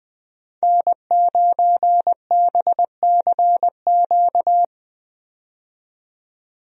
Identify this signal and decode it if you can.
Morse code, heard as a keyed tone, about 20 wpm, 700 Hz. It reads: N9BCQ
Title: USA Call Signs - 5 Characters - 20wpm